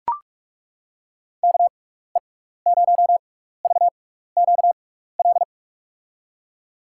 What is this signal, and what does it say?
Morse code, heard as a keyed tone, about 41 wpm, spaced as for 15 wpm, 700 Hz. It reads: KE0VQL
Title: USA Call Signs - 6x Character Spacing - 45wpm